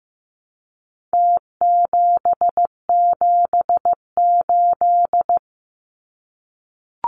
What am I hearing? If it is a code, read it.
Morse code, heard as a keyed tone, about 15 wpm, 700 Hz. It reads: T778
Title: Numbers - Contest Style - 4-Digit 15wpm